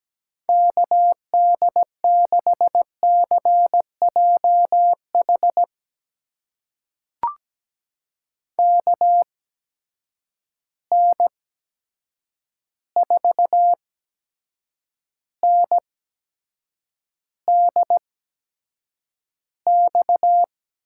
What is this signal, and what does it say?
Morse code, heard as a keyed tone, about 17 wpm, spaced as for 7 wpm, 700 Hz. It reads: KD6CJH KN4NDX
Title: USA Call Signs - 8x Character Spacing - 17wpm